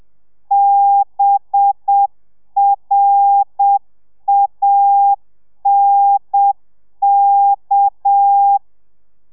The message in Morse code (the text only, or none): BRANK